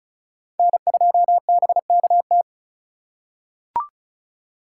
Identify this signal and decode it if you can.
Morse code, heard as a keyed tone, about 36 wpm, 700 Hz. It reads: N2BKT
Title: USA Call Signs - 5 Characters - 4x Character Spacing - 35wpm